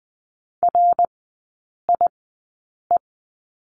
Morse code, heard as a keyed tone, about 20 wpm, spaced as for 8 wpm, 700 Hz. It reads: RIE
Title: 060 Lesson 11e - Letter h - Three Characters 20wpm